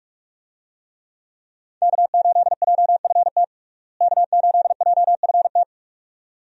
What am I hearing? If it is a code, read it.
Morse code, heard as a keyed tone, about 42 wpm, 700 Hz. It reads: K8JFT K8JFT
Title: USA Call Signs - 5 Characters - Repeated 2x - 45wpm